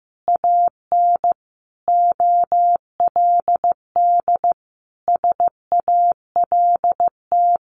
Morse code, heard as a keyed tone, about 15 wpm, 700 Hz. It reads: AN OLD SALT
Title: A Sound Of Thunder 15wpm